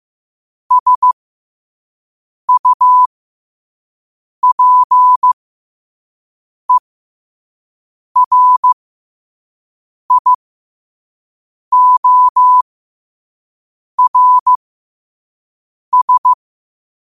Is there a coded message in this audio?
Yes: Morse code from 0.7 to 16.3 s, heard as a keyed tone, about 15 wpm, spaced as for 5 wpm, 1000 Hz. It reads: SUPERIORS